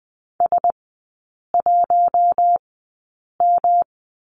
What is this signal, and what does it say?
Morse code, heard as a keyed tone, about 20 wpm, spaced as for 8 wpm, 700 Hz. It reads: S1M
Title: 107 Lesson 18e - Letter m - Three Characters 20wpm